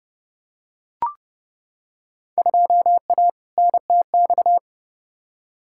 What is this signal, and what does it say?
Morse code, heard as a keyed tone, about 30 wpm, 700 Hz. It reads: 2A NTX
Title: Field Day Exchanges - 30wpm